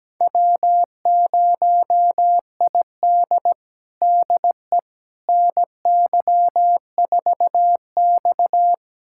Morse code, heard as a keyed tone, about 17 wpm, 700 Hz. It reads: W0ID DE NY4X
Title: USA Callsign DE Callsign - 4 Characters - 17wpm